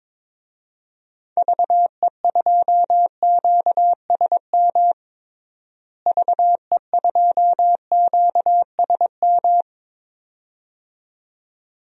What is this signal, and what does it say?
Morse code, heard as a keyed tone, about 22 wpm, 700 Hz. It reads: VE2QSM VE2QSM
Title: DXCC-100 Call Signs - 6 Characters - Repeated 2x - 22wpm